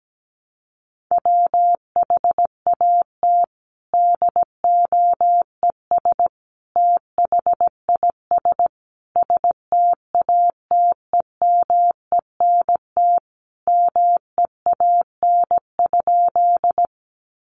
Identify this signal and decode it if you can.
Morse code, heard as a keyed tone, about 17 wpm, 700 Hz. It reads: WHAT DOES THIS STATEMENT MEAN?
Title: Sentences from Top 1200 Words 17wpm